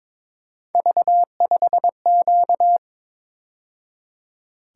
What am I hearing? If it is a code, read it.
Morse code, heard as a keyed tone, about 26 wpm, 700 Hz. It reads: V5Q
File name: Speed-Racing - Three Letter-Number Combinations - 22wpm